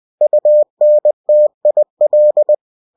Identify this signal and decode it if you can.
Morse code, heard as a keyed tone, about 20 wpm, 600 Hz. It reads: UNTIL
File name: cw-books / winnie-the-pooh / winnie-20wpm-page-0094